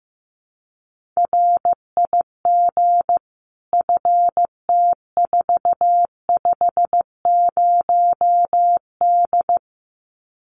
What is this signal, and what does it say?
Morse code, heard as a keyed tone, about 15 wpm, 700 Hz. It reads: RIG FT450D
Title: Common QSO Phrases 15wpm